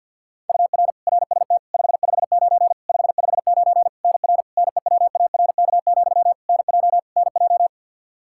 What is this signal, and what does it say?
Morse code, heard as a keyed tone, about 50 wpm, 700 Hz. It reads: UR RST 559 559 NR NEWARK, NJ NJ
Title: K1USN SST QSO Practice 50wpm